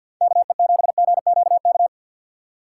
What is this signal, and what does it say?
Morse code, heard as a keyed tone, about 60 wpm, 700 Hz. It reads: KE7GQK
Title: Speed Racing - USA Call Signs 50wpm